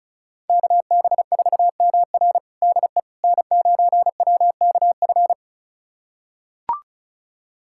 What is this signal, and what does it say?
Morse code, heard as a keyed tone, about 36 wpm, 700 Hz. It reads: KB4MR DE N9WKF
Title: USA Callsign DE Callsign - 5 Characters - 35wpm